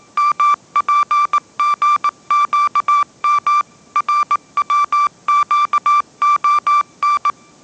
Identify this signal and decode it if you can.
Morse code, heard as a keyed tone, about 19 wpm, 1170 Hz. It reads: MPGQM RWQON
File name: mrx083 20wpm